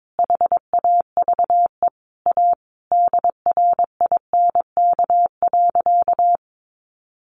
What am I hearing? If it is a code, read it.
Morse code, heard as a keyed tone, about 22 wpm, 700 Hz. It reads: HAVE A DRINK.
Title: Sentences from Top 1500 Words 22wpm